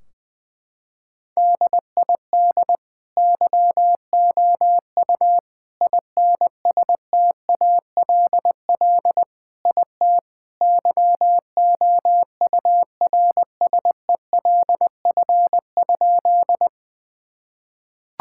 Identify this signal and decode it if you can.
Morse code, heard as a keyed tone, about 20 wpm, 700 Hz. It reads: DID YOU INSTALL IT YOURSELF?
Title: Sentences from Top 1600 Words 20wpm